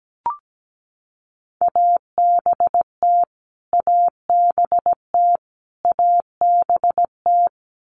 Morse code, heard as a keyed tone, about 17 wpm, 700 Hz. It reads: ABT ABT ABT
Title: Abbreviations Repeated 3x - 17wpm